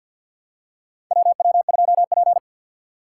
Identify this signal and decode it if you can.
Morse code, heard as a keyed tone, about 50 wpm, 700 Hz. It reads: WW2P
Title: K1USN SST QSO Practice - 50wpm